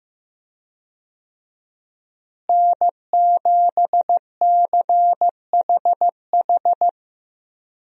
Morse code, heard as a keyed tone, about 15 wpm, 700 Hz. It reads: N7CHH
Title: USA Call Signs - 5 Characters - Repeated 2x - 15wpm